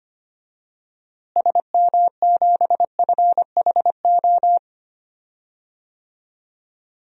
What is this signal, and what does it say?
Morse code, heard as a keyed tone, about 25 wpm, 700 Hz. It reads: SM7FHO
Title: DXCC-100 Call Signs - 6 Characters - Repeated 3x - 25wpm